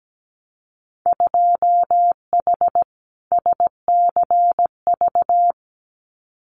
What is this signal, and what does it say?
Morse code, heard as a keyed tone, about 17 wpm, 700 Hz. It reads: 2H SCV
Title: Winter Field Day Exchanges Encore 17wpm